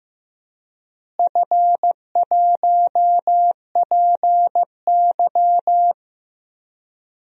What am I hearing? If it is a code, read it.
Morse code, heard as a keyed tone, about 15 wpm, 700 Hz. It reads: F1PY
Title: DXCC-100 Call Signs - 4 Characters - Repeated 3x - 15wpm